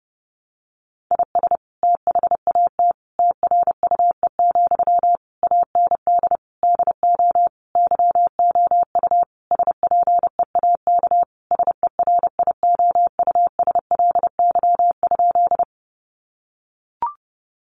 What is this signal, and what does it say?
Morse code, heard as a keyed tone, about 30 wpm, 700 Hz. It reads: IS THAT TRUE, AND DO YOU SPEAK SERIOUSLY?
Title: Sentences from Top 1200 Words 30wpm